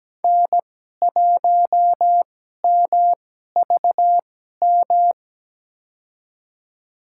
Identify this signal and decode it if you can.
Morse code, heard as a keyed tone, about 17 wpm, spaced as for 12 wpm, 700 Hz. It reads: N1MVM
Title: USA Call Signs - 2x Character Spacing - 17wpm